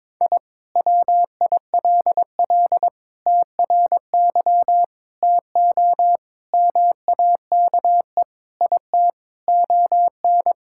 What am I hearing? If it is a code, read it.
Morse code, heard as a keyed tone, about 21 wpm, 700 Hz. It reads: I WILL TRY TO MAKE IT ON
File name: practice / 22wpm / Sentences with 3-Word Phrases - 22wpm